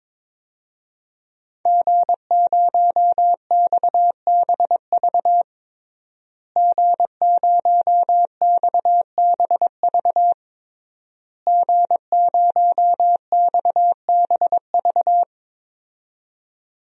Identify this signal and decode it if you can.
Morse code, heard as a keyed tone, about 22 wpm, 700 Hz. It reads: G0XBV G0XBV G0XBV
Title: DXCC-100 Call Signs - 5 Characters - Repeated 3x - 22wpm